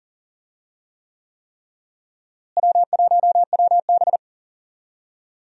Morse code, heard as a keyed tone, about 40 wpm, 700 Hz. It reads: W1WB
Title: USA Call Signs - 4 Characters - Repeated 3x - 40wpm